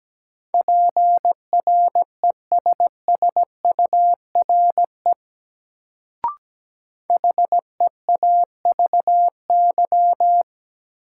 Morse code, heard as a keyed tone, about 17 wpm, 700 Hz. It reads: PRESSURE HEAVY
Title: Sentences from Top 1000 Words - Review Part 6 - New Word Drill - 17wpm